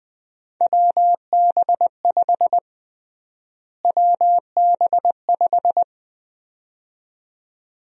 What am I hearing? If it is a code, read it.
Morse code, heard as a keyed tone, about 20 wpm, 700 Hz. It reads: WB5 WB5
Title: USA Call Sign Prefixes - Repeated 2x - 20wpm